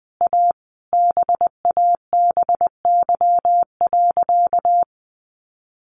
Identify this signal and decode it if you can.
Morse code, heard as a keyed tone, about 20 wpm, 700 Hz. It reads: A BABY.
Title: Sentences from Top 1400 Words 20wpm